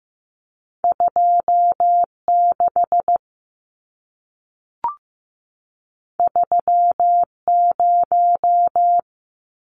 Morse code, heard as a keyed tone, about 15 wpm, 700 Hz. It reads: 2 6 3 0
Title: Numbers - 2-Digit 15wpm